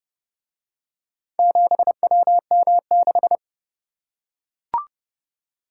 Morse code, heard as a keyed tone, about 30 wpm, 700 Hz. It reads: 7WM6